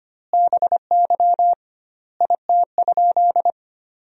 Morse code, heard as a keyed tone, about 25 wpm, 700 Hz. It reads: BY IT?
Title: 2x Word-Spacing - Sentences from Top 100 Words - 25wpm